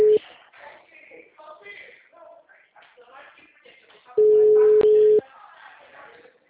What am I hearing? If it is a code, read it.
Ringback tone (Europe, 425 Hz), with speech.